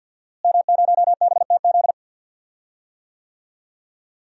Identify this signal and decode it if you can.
Morse code, heard as a keyed tone, about 46 wpm, 700 Hz. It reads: M0BTZ
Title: DXCC-100 Call Signs - 5 Characters - Repeated 3x - 50wpm